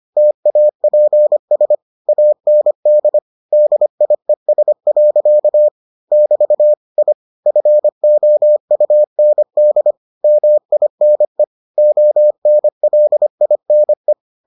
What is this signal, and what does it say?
Morse code, heard as a keyed tone, about 25 wpm, 600 Hz. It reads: TAPS AND DIES. <BT> I FOUND MINE ONLINE